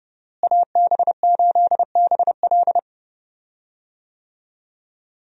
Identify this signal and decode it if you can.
Morse code, heard as a keyed tone, about 31 wpm, 700 Hz. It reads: AB8BL
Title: USA Call Signs - 5 Characters - Repeated 3x - 30wpm